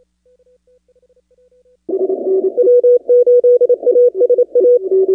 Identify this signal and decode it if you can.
Morse code, heard as a keyed tone, about 28 wpm, 490 Hz. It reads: W8ASA
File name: contests / 2019-cw-wpx-kt5j-k5pi- -2330 0-25